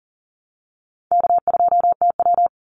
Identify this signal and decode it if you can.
Morse code, heard as a keyed tone, about 40 wpm, 700 Hz. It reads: K2TW
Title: Easy Call Signs - 40wpm